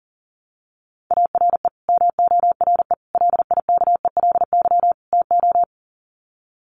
Morse code, heard as a keyed tone, about 40 wpm, 700 Hz. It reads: ARE MORE LIKELY TO
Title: Sets of 4 Words 40wpm